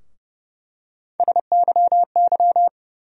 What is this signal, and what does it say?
Morse code, heard as a keyed tone, about 32 wpm, 700 Hz. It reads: SYY
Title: USA Call Sign Postfixes - 3 Letters 30wpm